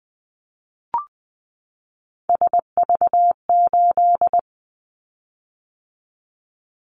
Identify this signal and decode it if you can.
Morse code, heard as a keyed tone, about 20 wpm, 700 Hz. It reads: SV8
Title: Callsign Prefixes for Top 50 DXCC - 20wpm